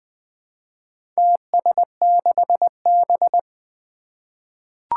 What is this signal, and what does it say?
Morse code, heard as a keyed tone, about 20 wpm, 700 Hz. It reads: TS6B
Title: Four Challenging Letter-Number Combinations 20wpm